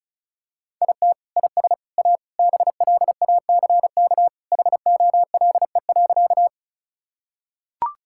Morse code, heard as a keyed tone, about 34 wpm, 700 Hz. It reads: IT IS A BLACK HOLE.
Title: Sentences from Top 1600 Words 35wpm